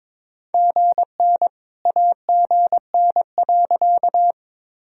Morse code, heard as a keyed tone, about 21 wpm, 700 Hz. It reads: GN AGN.
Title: Sentences from Top 300 Words and Abbreviations 22wpm